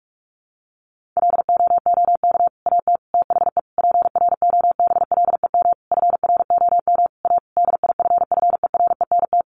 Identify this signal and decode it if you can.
Morse code, heard as a keyed tone, about 44 wpm, 700 Hz. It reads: LOOK AT THE PROBLEM FROM A DIFFERENT
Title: Sentences from Top 1500 Words 45wpm